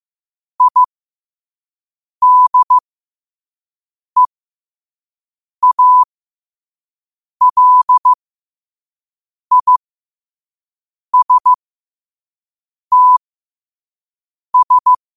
Morse code, heard as a keyed tone, about 15 wpm, spaced as for 5 wpm, 1000 Hz. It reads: IDEALISTS
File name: qotdfiles / 5wpm